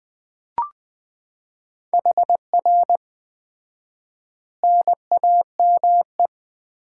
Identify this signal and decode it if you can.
Morse code, heard as a keyed tone, about 20 wpm, 700 Hz. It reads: HR NAME
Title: Common QSO Phrases - 4x Word-Spacing - 20wpm